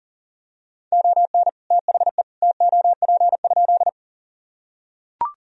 Morse code, heard as a keyed tone, about 40 wpm, 700 Hz. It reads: ON THE TOP?
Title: Sentences from Top 1000 Words - Review Part 2 - 40wpm